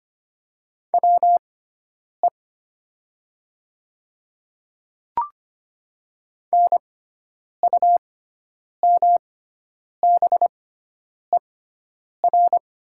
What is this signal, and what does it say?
Morse code, heard as a keyed tone, about 25 wpm, spaced as for 9 wpm, 700 Hz. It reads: WE NUMBER
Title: 6x Character Spacing - Top 100 Words - 25wpm